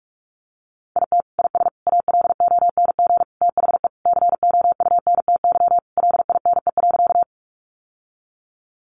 Morse code, heard as a keyed tone, about 44 wpm, 700 Hz. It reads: IT IS ALONG THE COUNTY LINE.
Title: Sentences from Top 1200 Words 45wpm